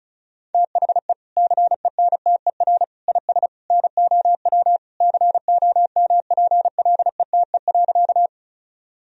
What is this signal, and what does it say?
Morse code, heard as a keyed tone, about 34 wpm, 700 Hz. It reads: THE CENTER IS NOW COMPLETE.
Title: Sentences from Top 1000 Words - Review Part 2 - 35wpm